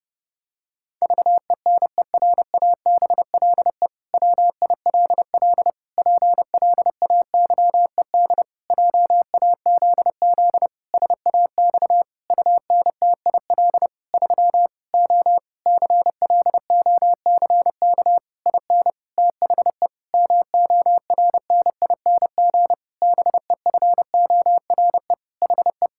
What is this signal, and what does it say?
Morse code, heard as a keyed tone, about 31 wpm, 700 Hz. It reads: VENERABLE WILL PLAYED JAZZ SAX UNTIL 3 O CLOCK IN THE MORNING BEFORE HE
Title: Pangram Challenge v2 - 30wpm